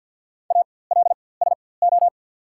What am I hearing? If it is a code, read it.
Morse code, heard as a keyed tone, about 54 wpm, spaced as for 24 wpm, 690 Hz. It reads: ALSO
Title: 4x Character Spacing - Top 500 Words - 50wpm